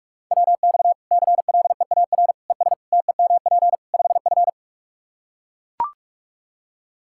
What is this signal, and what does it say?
Morse code, heard as a keyed tone, about 42 wpm, 700 Hz. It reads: WX CLEAR ES TEMP 5F
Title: Practice QSO - Exchange 2 - Info - 45wpm